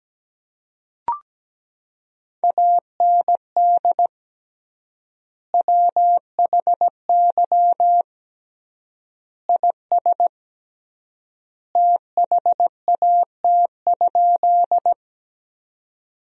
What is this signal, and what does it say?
Morse code, heard as a keyed tone, about 17 wpm, 700 Hz. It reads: AND WHY IS THAT?
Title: 3x Word-Spacing - Sentences from Top 200 Words - 17wpm